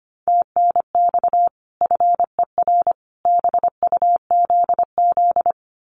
Morse code, heard as a keyed tone, about 25 wpm, 700 Hz. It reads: TNX FER BUZZ
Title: Common QSO Phrases 25wpm